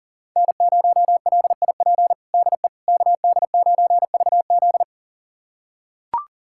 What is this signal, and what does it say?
Morse code, heard as a keyed tone, about 40 wpm, 700 Hz. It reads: N0LIP DE KD9VZ
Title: USA Callsign DE Callsign - 5 Characters - 40wpm